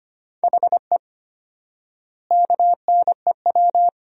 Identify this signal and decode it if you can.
Morse code, heard as a keyed tone, about 25 wpm, 700 Hz. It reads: HE KNEW
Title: 4x Word-Spacing - Sentences from Top 500 Words - 25wpm